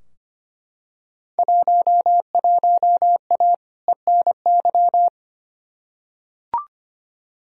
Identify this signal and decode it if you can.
Morse code, heard as a keyed tone, about 25 wpm, 700 Hz. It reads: 11A ENY
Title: Field Day Exchanges Encore - 25wpm